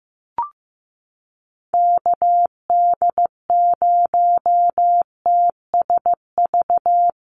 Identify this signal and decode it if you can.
Morse code, heard as a keyed tone, about 15 wpm, 700 Hz. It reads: KD0TSV
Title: USA Call Signs Repeated 2x 15wpm